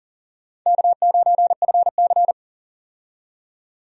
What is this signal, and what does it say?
Morse code, heard as a keyed tone, about 40 wpm, 700 Hz. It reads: K9FC
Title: USA Call Signs - 4 Characters - 40wpm